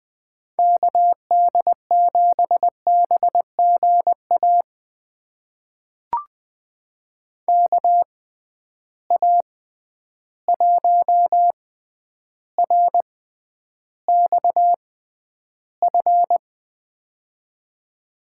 Morse code, heard as a keyed tone, about 20 wpm, spaced as for 10 wpm, 700 Hz. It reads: KD7BGA KA1RXF
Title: USA Call Signs - 6 Characters - 6x Character Spacing - 20wpm